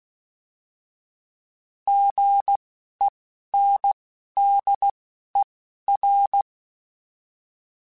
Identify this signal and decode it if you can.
Morse code, heard as a keyed tone, about 16 wpm, spaced as for 11 wpm, 800 Hz. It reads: GENDER
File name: Alphabetical / Alphabetical mp3s / Gs @10